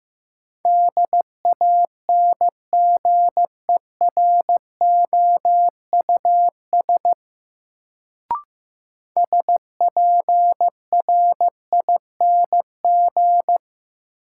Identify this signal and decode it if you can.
Morse code, heard as a keyed tone, about 15 wpm, 700 Hz. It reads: DANGEROUS SPRING
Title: Sentences from Top 1200 Words - New Word Drill - 15wpm